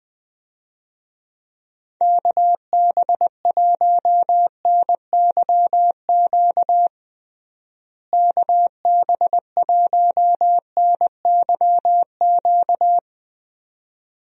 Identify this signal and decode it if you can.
Morse code, heard as a keyed tone, about 21 wpm, 700 Hz. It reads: KB1NYQ KB1NYQ
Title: USA Call Signs - 6 Characters - Repeated 2x - 20wpm